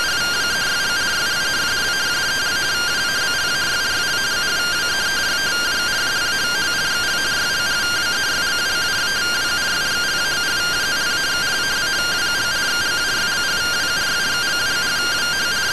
Two-tone teleprinter signal, about 170 Hz shift.